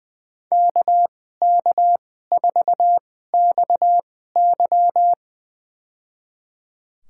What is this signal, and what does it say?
Morse code, heard as a keyed tone, about 21 wpm, spaced as for 15 wpm, 700 Hz. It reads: KK4XY
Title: USA Call Signs - 2x Character Spacing - 20wpm